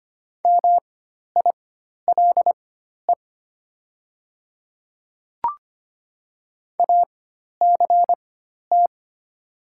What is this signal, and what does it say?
Morse code, heard as a keyed tone, about 25 wpm, spaced as for 12 wpm, 700 Hz. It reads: MILE ACT
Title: 4x Character Spacing - Top 300 Words - 25wpm